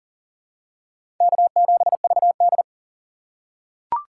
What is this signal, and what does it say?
Morse code, heard as a keyed tone, about 40 wpm, 700 Hz. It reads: K7VD
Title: USA Call Signs - 4 Characters - 4x Character Spacing - 40wpm